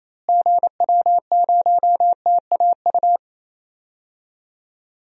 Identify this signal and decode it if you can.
Morse code, heard as a keyed tone, about 28 wpm, 700 Hz. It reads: GW0TAU
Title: DXCC-100 Call Signs - 6 Characters - Repeated 3x - 28wpm